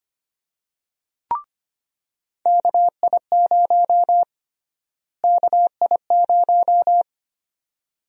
Morse code, heard as a keyed tone, about 25 wpm, 700 Hz. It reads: KI0 KI0
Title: USA Call Sign Prefixes - Repeated 2x - 25wpm